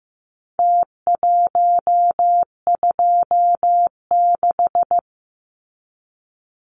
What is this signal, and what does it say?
Morse code, heard as a keyed tone, about 15 wpm, 700 Hz. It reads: T126
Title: Numbers - Contest Style - 4-Digit 15wpm